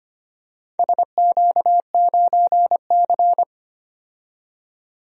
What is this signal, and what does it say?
Morse code, heard as a keyed tone, about 25 wpm, 700 Hz. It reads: SQ9C
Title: DXCC-100 Call Signs - 4 Characters - 25wpm